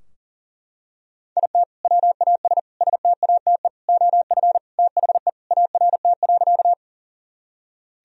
Morse code, heard as a keyed tone, about 40 wpm, 700 Hz. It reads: IT WAS STATE OF THE ART.
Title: Sentences from Top 1000 Words - Review Part 3 - 40wpm